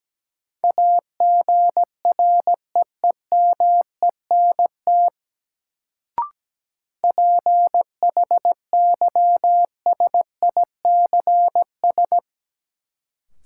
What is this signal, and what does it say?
Morse code, heard as a keyed tone, about 17 wpm, 700 Hz. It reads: AGREEMENT PHYSICS
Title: Sentences from Top 1500 Words - New Word Drill Alt - 17wpm